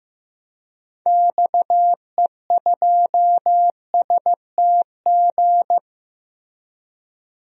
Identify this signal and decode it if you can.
Morse code, heard as a keyed tone, about 15 wpm, 700 Hz. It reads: XE2STG
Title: DXCC-100 Call Signs - 6 Characters - Repeated 2x - 15wpm